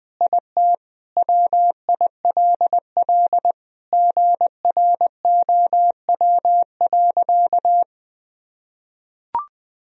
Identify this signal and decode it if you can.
Morse code, heard as a keyed tone, about 21 wpm, 700 Hz. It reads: IT WILL GROW.
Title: Sentences from Top 1000 Words - Review Part 2 - 20wpm